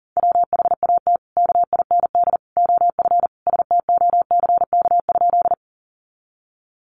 Morse code, heard as a keyed tone, about 40 wpm, 700 Hz. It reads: WHAT KIND OF STOCK?
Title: Sentences from Top 800 Words 40wpm